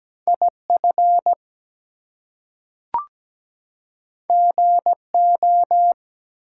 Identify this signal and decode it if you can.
Morse code, heard as a keyed tone, about 17 wpm, 700 Hz. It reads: IF GO
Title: Two Letter Words - 17wpm